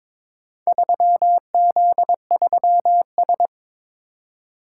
Morse code, heard as a keyed tone, about 22 wpm, 700 Hz. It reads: 3Z3S
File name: practice / 22wpm / Four Letter-Number Combinations - 22wpm